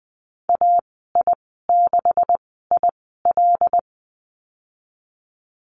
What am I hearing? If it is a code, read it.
Morse code, heard as a keyed tone, about 20 wpm, spaced as for 15 wpm, 700 Hz. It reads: AI6IL
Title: 154 Lesson 25g - Letter p - Call Signs 20wpm